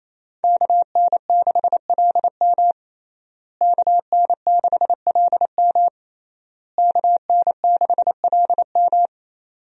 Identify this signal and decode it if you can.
Morse code, heard as a keyed tone, about 27 wpm, 700 Hz. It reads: KN6LM KN6LM KN6LM
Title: USA Call Signs - 5 Characters - Repeated 3x - 28wpm